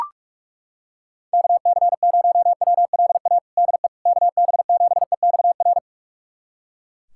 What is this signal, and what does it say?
Morse code, heard as a keyed tone, about 43 wpm, 700 Hz. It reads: KC0WLA DE KB7EXR
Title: USA Callsign DE Callsign 45wpm